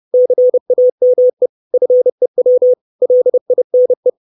Morse code, heard as a keyed tone, about 30 wpm, 500 Hz. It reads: CAME FEW LINE
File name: 632 Words 30WPM